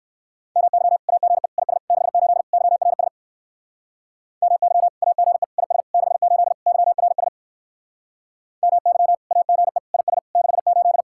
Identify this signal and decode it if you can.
Morse code, heard as a keyed tone, about 69 wpm, 700 Hz. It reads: MY AGE IS 68 YRS MY AGE IS 68 YRS MY AGE IS 68